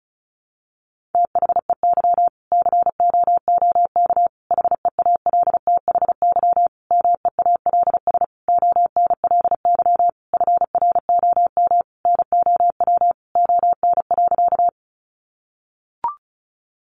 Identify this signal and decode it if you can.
Morse code, heard as a keyed tone, about 34 wpm, 700 Hz. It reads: THEY COOK HEALTHY MEALS ONLY FROM NOW ON.
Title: Easy Phrases 35wpm